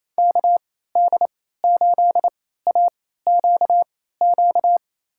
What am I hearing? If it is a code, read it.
Morse code, heard as a keyed tone, about 28 wpm, spaced as for 16 wpm, 700 Hz. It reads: KD8AQQ
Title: USA Call Signs - 6 Characters - 3x Character Spacing - 28wpm